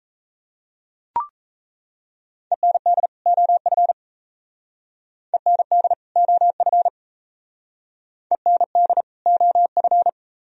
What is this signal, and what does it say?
Morse code, heard as a keyed tone, about 35 wpm, 700 Hz. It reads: END OF END OF END OF
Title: Speed-Racing - Sets of 2 Words 28wpm